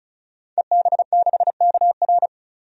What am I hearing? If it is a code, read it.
Morse code, heard as a keyed tone, about 34 wpm, 700 Hz. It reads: EB6KR
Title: DXCC-100 Call Signs - 5 Characters - 35wpm